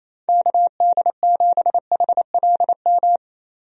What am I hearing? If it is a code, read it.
Morse code, heard as a keyed tone, about 27 wpm, 700 Hz. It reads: KD7HLM
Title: USA Call Signs - 6 Characters - 28wpm